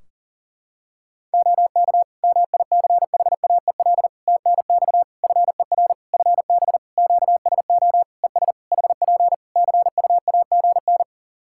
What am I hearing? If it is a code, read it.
Morse code, heard as a keyed tone, about 37 wpm, 700 Hz. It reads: OK MICHAEL TNX FER FB QSO ES HP CUAGN
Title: K1USN SST QSO Practice 40wpm